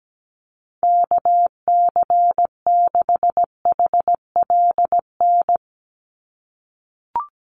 Morse code, heard as a keyed tone, about 17 wpm, 700 Hz. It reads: KC6HLN
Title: USA Call Signs 17wpm